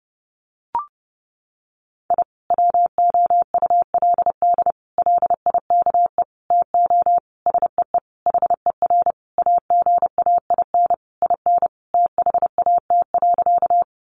Morse code, heard as a keyed tone, about 30 wpm, 700 Hz. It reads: I WOULD LIKE TO SEE HER AGAIN IN THAT.
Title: Sentences from Top 200 Words 30wpm